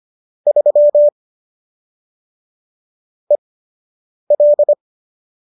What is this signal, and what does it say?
Morse code, heard as a keyed tone, about 25 wpm, spaced as for 8 wpm, 590 Hz. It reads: 3 EL